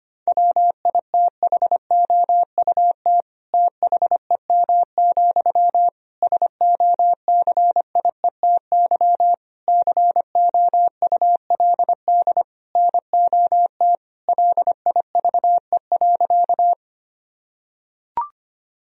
Morse code, heard as a keyed tone, about 25 wpm, 700 Hz. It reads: WITHOUT THEM, SOCIETY COULD NOT LIVE.
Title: Sentences from Top 1000 Words - Review Part 4 - 25wpm